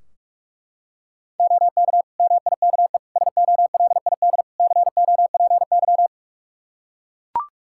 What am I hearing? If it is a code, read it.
Morse code, heard as a keyed tone, about 41 wpm, 700 Hz. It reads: OK MIKE SOLID COPY
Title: Practice QSO - Exchange 3 - More Info - 45wpm